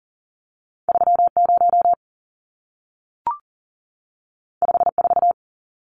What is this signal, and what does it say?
Morse code, heard as a keyed tone, about 40 wpm, 700 Hz. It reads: 30 54
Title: Numbers - 2-Digit 40wpm